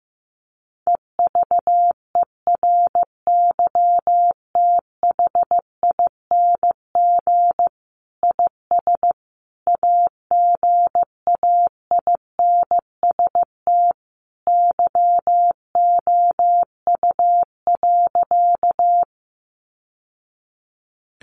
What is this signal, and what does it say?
Morse code, heard as a keyed tone, about 15 wpm, 700 Hz. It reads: EVERYTHING IS AGAINST YOU.
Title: Sentences from Top 700 Words 15wpm